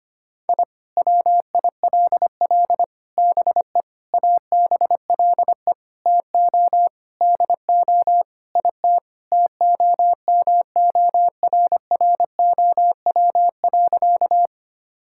Morse code, heard as a keyed tone, about 25 wpm, 700 Hz. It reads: I WILL BE ABLE TO DO IT TOMORROW.
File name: practice / 25wpm / Sentences with 4-Word Phrases - 25wpm